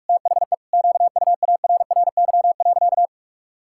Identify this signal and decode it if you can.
Morse code, heard as a keyed tone, about 41 wpm, 700 Hz. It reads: THE QUARRY.
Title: Pangram Challenge v2 - 45wpm